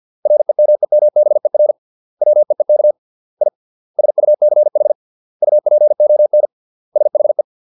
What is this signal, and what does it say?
Morse code, heard as a keyed tone, about 50 wpm, 600 Hz. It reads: REMEMBER WEEK I SUCH UPON SHE
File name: 632 words / 632 Words 50WPM